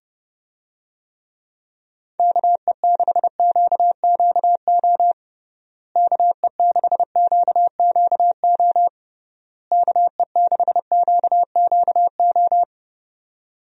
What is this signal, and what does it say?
Morse code, heard as a keyed tone, about 31 wpm, 700 Hz. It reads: KE6QQO KE6QQO KE6QQO
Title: USA Call Signs - 6 Characters - Repeated 3x - 30wpm